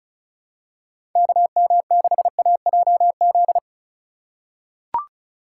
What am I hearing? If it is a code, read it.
Morse code, heard as a keyed tone, about 36 wpm, 700 Hz. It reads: KM6AJZ
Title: USA Call Signs - 2x Character Spacing - 35wpm